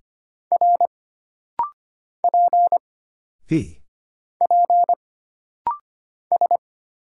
Morse code, heard as a keyed tone, about 25 wpm, 700 Hz. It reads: R P P S
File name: Single Character (Letter:Number:Punctuation:Prosign) - Rapid-Fire - 25wpm